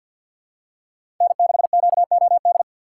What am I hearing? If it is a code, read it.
Morse code, heard as a keyed tone, about 48 wpm, 700 Hz. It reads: N6QOD
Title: USA Call Signs - 4x Character Spacing - 50wpm